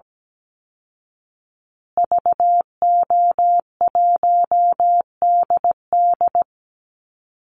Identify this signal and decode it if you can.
Morse code, heard as a keyed tone, about 17 wpm, 700 Hz. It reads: VO1DD
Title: Call Signs Repeated 2x - 17wpm